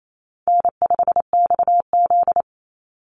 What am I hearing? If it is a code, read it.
Morse code, heard as a keyed tone, about 28 wpm, 700 Hz. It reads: N5XZ
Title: Easy Call Signs - 28wpm